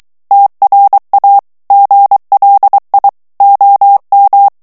Morse code, heard as a keyed tone, about 23 wpm, 800 Hz. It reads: TRA GLI OM
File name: ITALIEN texte clair 2 vitesse 22 mots minute